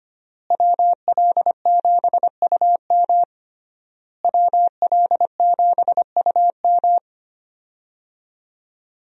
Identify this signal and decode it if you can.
Morse code, heard as a keyed tone, about 25 wpm, 700 Hz. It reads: WL7UM WL7UM
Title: USA Call Signs - 5 Characters - Repeated 3x - 25wpm